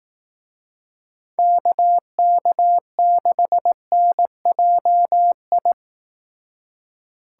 Morse code, heard as a keyed tone, about 18 wpm, 700 Hz. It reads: KK6NJI
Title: Speed Racing - USA 2x3 Call Signs 15wpm